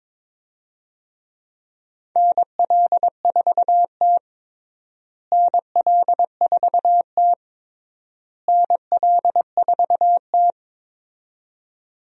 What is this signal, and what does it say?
Morse code, heard as a keyed tone, about 22 wpm, 700 Hz. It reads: NL4T NL4T NL4T
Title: USA Call Signs - 4 Characters - Repeated 3x - 22wpm